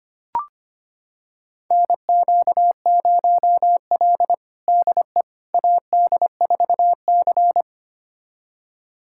Morse code, heard as a keyed tone, about 25 wpm, 700 Hz. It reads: NQ0L DE AD4C
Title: USA Callsign DE Callsign - 4 Characters - 25wpm